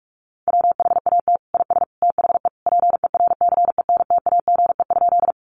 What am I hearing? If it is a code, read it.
Morse code, heard as a keyed tone, about 44 wpm, 700 Hz. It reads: WHAT IS THE PERCENTAGE?
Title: Sentences from Top 1300 Words 45wpm